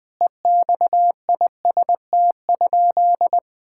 Morse code, heard as a keyed tone, about 20 wpm, 700 Hz. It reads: EXIST?
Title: Sentences from Top 1000 Words - Review Part 5 - 20wpm